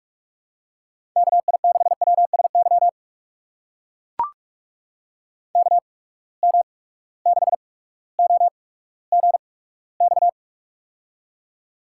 Morse code, heard as a keyed tone, about 44 wpm, spaced as for 19 wpm, 700 Hz. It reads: KI6WSY KM6OGX
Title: USA Call Signs - 6 Characters - 8x Character Spacing - 45wpm